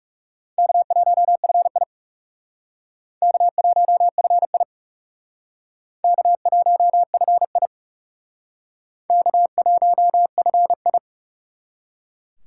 Morse code, heard as a keyed tone, about 37 wpm, 700 Hz. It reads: K1FI K1FI K1FI K1FI
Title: Speed Racing - USA 1x2 Call Signs 30wpm